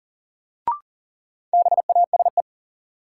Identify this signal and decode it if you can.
Morse code, heard as a keyed tone, about 40 wpm, 700 Hz. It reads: BASE
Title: Sentences from Top 1600 Words - New Word Drill Alt - 40wpm